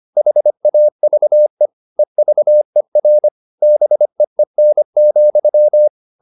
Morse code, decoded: HAVE EVER BEEN,

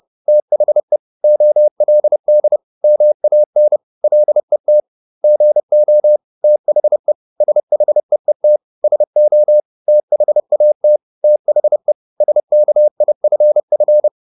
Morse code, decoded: THE OLD MAN LET GO THE SHEET SO THAT THE SKIFF

